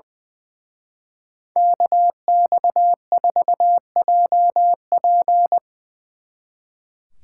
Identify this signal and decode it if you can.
Morse code, heard as a keyed tone, about 23 wpm, 700 Hz. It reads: KX4JP
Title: Speed Racing - USA Call Signs 20wpm